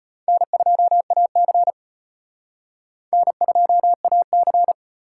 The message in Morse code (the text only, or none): N2AC N2AC